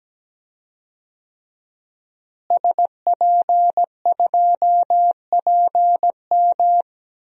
Morse code, heard as a keyed tone, about 17 wpm, 700 Hz. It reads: SP2PM
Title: DXCC-100 Call Signs - 5 Characters - Repeated 3x - 17wpm